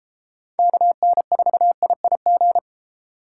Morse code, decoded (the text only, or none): KN4IIG